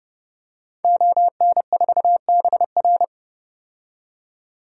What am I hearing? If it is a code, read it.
Morse code, heard as a keyed tone, about 30 wpm, 700 Hz. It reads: ON4BR
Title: DXCC-100 Call Signs - 5 Characters - 30wpm